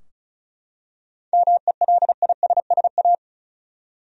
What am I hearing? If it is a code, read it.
Morse code, heard as a keyed tone, about 36 wpm, 700 Hz. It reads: MELISSA
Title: USA Common Names - 35wpm